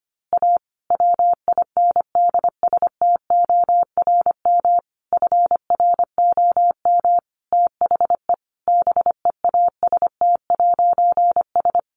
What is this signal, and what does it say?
Morse code, heard as a keyed tone, about 25 wpm, 700 Hz. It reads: A WINDSTORM FROM THE BEAST'S
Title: A Sound Of Thunder 25wpm